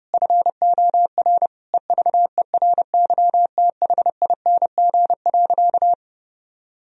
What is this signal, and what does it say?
Morse code, heard as a keyed tone, about 32 wpm, 700 Hz. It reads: FOR EVERYTHING.
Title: Sentences from Top 1000 Words - Review Part 3 - 30wpm